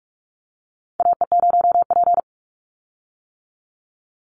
Morse code, heard as a keyed tone, about 46 wpm, 700 Hz. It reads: AE0P
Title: Call Signs - 45wpm